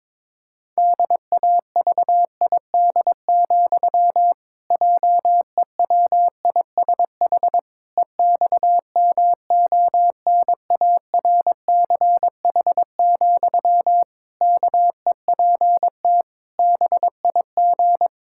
Morse code, heard as a keyed tone, about 22 wpm, 700 Hz. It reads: DAVID, JEWISH EXMONARCH, KEPT BIG
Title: Pangram Challenge v2 - 22wpm